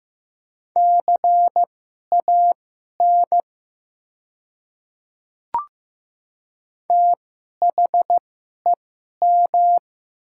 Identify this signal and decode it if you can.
Morse code, heard as a keyed tone, about 15 wpm, spaced as for 11 wpm, 700 Hz. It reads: CAN THEM
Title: 2x Character Spacing - Top 100 Words - 15wpm